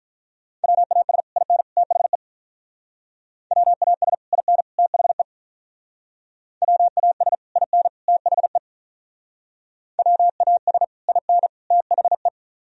Morse code, heard as a keyed tone, about 43 wpm, 700 Hz. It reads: WAS IN THE WAS IN THE WAS IN THE WAS IN THE
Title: Speed-Racing - Sets of 3 Words 35wpm